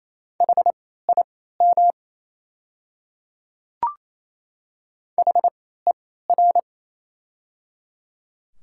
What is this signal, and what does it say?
Morse code, heard as a keyed tone, about 28 wpm, spaced as for 16 wpm, 690 Hz. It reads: HIM HER
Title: 3x Character Spacing - Top 100 Words - 28wpm